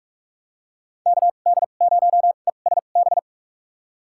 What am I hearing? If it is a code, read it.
Morse code, heard as a keyed tone, about 43 wpm, spaced as for 32 wpm, 700 Hz. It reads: KD0ESB
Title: USA Call Signs - 2x Character Spacing - 45wpm